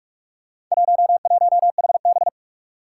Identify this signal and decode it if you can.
Morse code, heard as a keyed tone, about 46 wpm, 700 Hz. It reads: 11HB
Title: Four Challenging Letter-Number Combinations 45wpm